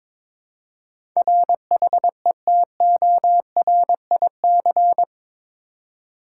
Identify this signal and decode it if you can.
Morse code, heard as a keyed tone, about 22 wpm, 700 Hz. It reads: RHETORIC
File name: practice / 22wpm / Eight Letter Words - 22wpm